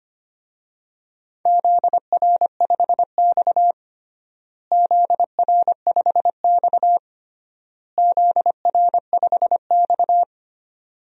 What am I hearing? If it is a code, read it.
Morse code, heard as a keyed tone, about 25 wpm, 700 Hz. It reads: ZR5X ZR5X ZR5X
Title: DXCC-100 Call Signs - 4 Characters - Repeated 3x - 25wpm